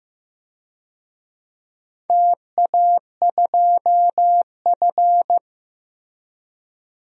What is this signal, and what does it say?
Morse code, heard as a keyed tone, about 15 wpm, 700 Hz. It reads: TA2F
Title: DXCC-100 Call Signs - 4 Characters - Repeated 2x - 15wpm